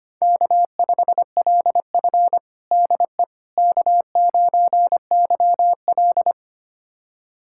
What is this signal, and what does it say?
Morse code, heard as a keyed tone, about 25 wpm, 700 Hz. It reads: K5LF DE K9YL
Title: USA Callsign DE Callsign - 4 Characters - 25wpm